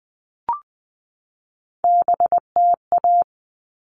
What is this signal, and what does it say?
Morse code, heard as a keyed tone, about 20 wpm, 700 Hz. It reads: BTA
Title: All Cut Numbers - 3-Digit 20wpm